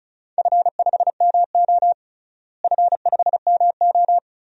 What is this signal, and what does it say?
Morse code, heard as a keyed tone, about 36 wpm, 700 Hz. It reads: F5MO F5MO
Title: DXCC-100 Call Signs - 4 Characters - Repeated 2x - 35wpm